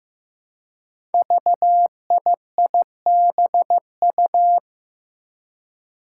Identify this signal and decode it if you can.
Morse code, heard as a keyed tone, about 15 wpm, 700 Hz. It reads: VIIBU
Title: Five Challenging Letter-Number Combinations 15wpm